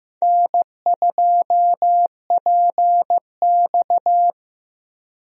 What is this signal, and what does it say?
Morse code, heard as a keyed tone, about 15 wpm, 700 Hz. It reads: N2PX
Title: USA Call Signs - 4 Characters - Repeated 3x - 15wpm